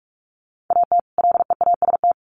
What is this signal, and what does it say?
Morse code, heard as a keyed tone, about 44 wpm, 700 Hz. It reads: AT LEAST
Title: Sets of 2 Words 45wpm